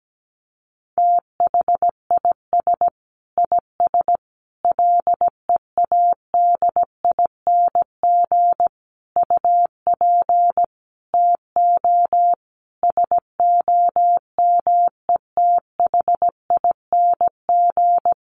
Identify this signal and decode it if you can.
Morse code, heard as a keyed tone, about 17 wpm, 700 Hz. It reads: THIS IS LEADING UP TO SOMETHING
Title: Sentences from Top 1100 Words 17wpm